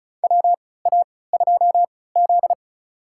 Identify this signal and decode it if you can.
Morse code, heard as a keyed tone, about 36 wpm, spaced as for 20 wpm, 700 Hz. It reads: WA2Z
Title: USA Call Signs - 4 Characters - 3x Character Spacing - 35wpm